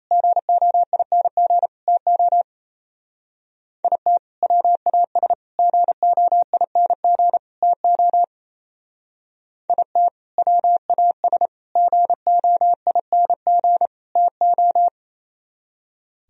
Morse code, heard as a keyed tone, about 33 wpm, 700 Hz. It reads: GOING TO IT WAS GOING TO IT WAS GOING TO
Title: Speed-Racing - Sets of 4 Words 28wpm